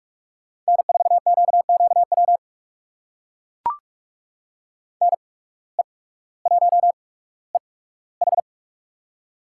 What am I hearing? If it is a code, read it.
Morse code, heard as a keyed tone, about 42 wpm, spaced as for 19 wpm, 700 Hz. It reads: N4QQW NE1EH
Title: USA Call Signs - 5 Characters - 8x Character Spacing - 45wpm